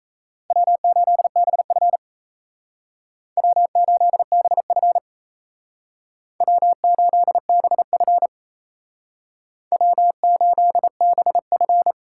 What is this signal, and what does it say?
Morse code, heard as a keyed tone, about 34 wpm, 700 Hz. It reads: W8BF W8BF W8BF W8BF
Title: Speed Racing - USA 1x2 Call Signs 28wpm